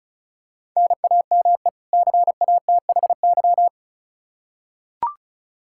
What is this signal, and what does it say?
Morse code, heard as a keyed tone, about 36 wpm, 700 Hz. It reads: NAME CATHY
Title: Practice QSO - Exchange 1 - Beginning - 35wpm